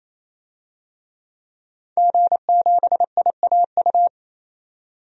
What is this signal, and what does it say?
Morse code, heard as a keyed tone, about 27 wpm, 700 Hz. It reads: G7IAU
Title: DXCC-100 Call Signs - 5 Characters - 28wpm